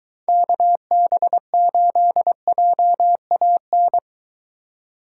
Morse code, heard as a keyed tone, about 22 wpm, 700 Hz. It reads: KB8JAN